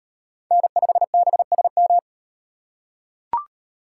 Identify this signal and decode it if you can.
Morse code, heard as a keyed tone, about 33 wpm, 700 Hz. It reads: N5BSM